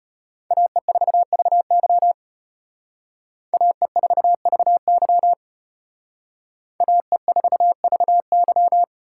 Morse code, heard as a keyed tone, about 32 wpm, 700 Hz. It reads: AE4VY AE4VY AE4VY